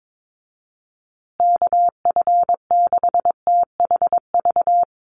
Morse code, heard as a keyed tone, about 22 wpm, 700 Hz. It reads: KF6THV
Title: USA Call Signs Repeated 2x 22wpm